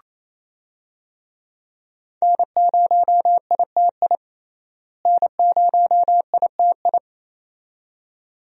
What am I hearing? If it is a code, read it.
Morse code, heard as a keyed tone, about 27 wpm, 700 Hz. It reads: N0ITI N0ITI
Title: USA Call Signs - 5 Characters - Repeated 2x - 28wpm